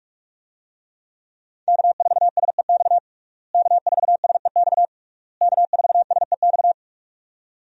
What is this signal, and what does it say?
Morse code, heard as a keyed tone, about 46 wpm, 700 Hz. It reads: K4SEX K4SEX K4SEX
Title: USA Call Signs - 5 Characters - Repeated 3x - 45wpm